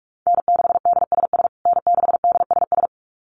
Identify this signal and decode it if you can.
Morse code, heard as a keyed tone, about 44 wpm, 700 Hz. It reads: N6DSS N6DSS
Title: USA Call Signs Repeated 2x 45wpm